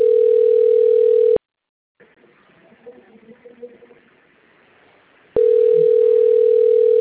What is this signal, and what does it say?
Ringback tone (North America), 440 + 480 Hz.